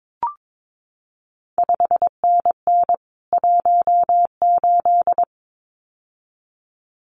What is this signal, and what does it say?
Morse code, heard as a keyed tone, about 22 wpm, 700 Hz. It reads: 5NN 18
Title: CQ WW DX Contest Exchange Encore - 22wpm